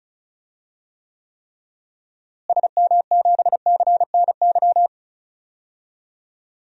Morse code, heard as a keyed tone, about 36 wpm, 700 Hz. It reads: SM7CNY
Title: DXCC-100 Call Signs - 6 Characters - Repeated 3x - 35wpm